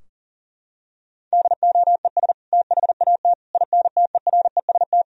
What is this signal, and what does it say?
Morse code, heard as a keyed tone, about 40 wpm, 700 Hz. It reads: DOES THAT INTEREST
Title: Sentences from Top 1000 Words - Review Part 2 - 40wpm